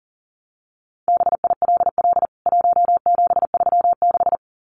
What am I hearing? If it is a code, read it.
Morse code, heard as a keyed tone, about 40 wpm, 700 Hz. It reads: BILL 1736
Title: CWT Exchange 40wpm